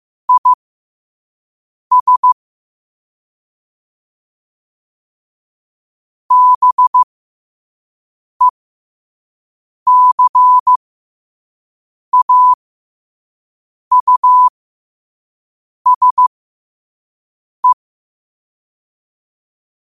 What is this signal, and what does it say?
Morse code, heard as a keyed tone, about 15 wpm, spaced as for 5 wpm, 1000 Hz. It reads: IS BECAUSE
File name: qotdfiles / 5wpm